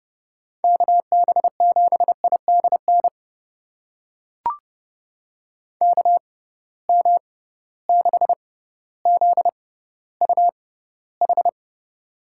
Morse code, heard as a keyed tone, about 30 wpm, spaced as for 15 wpm, 700 Hz. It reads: KB7IDN KM6ZUH
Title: USA Call Signs - 6 Characters - 6x Character Spacing - 30wpm